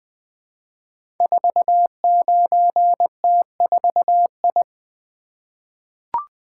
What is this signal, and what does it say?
Morse code, heard as a keyed tone, about 20 wpm, 700 Hz. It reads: 49T4I
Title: Code Groups - 5 Letter:Numbers - 20wpm